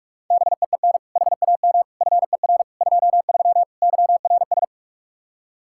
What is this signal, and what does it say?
Morse code, heard as a keyed tone, about 54 wpm, 700 Hz. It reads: BEEN HAM FER 23 YRS
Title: Speed-Racing Common QSO Phrases - 45wpm